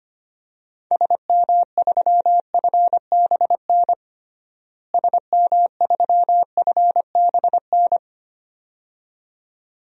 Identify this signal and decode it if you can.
Morse code, heard as a keyed tone, about 25 wpm, 700 Hz. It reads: SM3FBN SM3FBN
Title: DXCC-100 Call Signs - 6 Characters - Repeated 2x - 25wpm